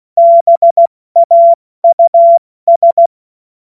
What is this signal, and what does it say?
Morse code, heard as a keyed tone, about 16 wpm, 680 Hz. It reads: BAUS
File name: Morsekurs / Witze / Witz 30. Dezember 2025 70 F80